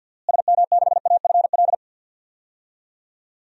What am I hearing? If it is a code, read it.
Morse code, heard as a keyed tone, about 47 wpm, 700 Hz. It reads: SM6AFL
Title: DXCC-100 Call Signs - 6 Characters - Repeated 3x - 50wpm